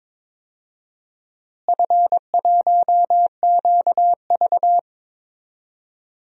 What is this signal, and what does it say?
Morse code, heard as a keyed tone, about 22 wpm, 700 Hz. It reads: F1QV
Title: DXCC-100 Call Signs - 4 Characters - Repeated 3x - 22wpm